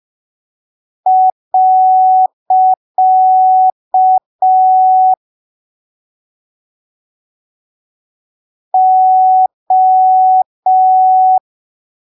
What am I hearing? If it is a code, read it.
Morse code, heard as a keyed tone, about 5 wpm, 750 Hz. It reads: . O
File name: arquivos / texto / texto 30 05ppm